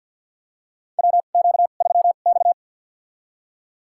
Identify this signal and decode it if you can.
Morse code, heard as a keyed tone, about 51 wpm, spaced as for 36 wpm, 700 Hz. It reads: WQ3X
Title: USA Call Signs - 4 Characters - 2x Character Spacing - 50wpm